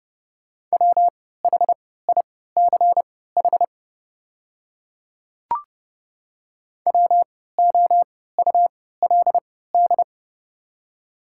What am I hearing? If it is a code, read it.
Morse code, heard as a keyed tone, about 31 wpm, spaced as for 17 wpm, 700 Hz. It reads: WHICH WOULD
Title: 3x Character Spacing - Top 100 Words - 30wpm